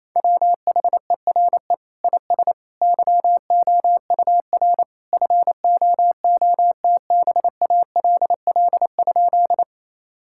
Morse code, heard as a keyed tone, about 28 wpm, 700 Hz. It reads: WHERE IS YOUR FOOTBALL?
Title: Sentences from Top 1600 Words 28wpm